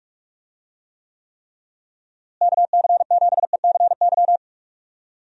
Morse code, heard as a keyed tone, about 41 wpm, 700 Hz. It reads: KC7ECY
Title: USA Call Signs - 6 Characters - Repeated 3x - 45wpm